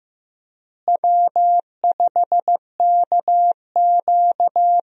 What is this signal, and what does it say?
Morse code, heard as a keyed tone, about 15 wpm, 700 Hz. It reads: W5KQ
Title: USA Call Signs - 4 Characters - Repeated 3x - 15wpm